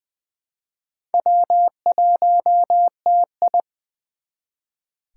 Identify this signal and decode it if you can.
Morse code, heard as a keyed tone, about 24 wpm, 700 Hz. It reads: W1TI
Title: Speed Racing - USA Call Signs 20wpm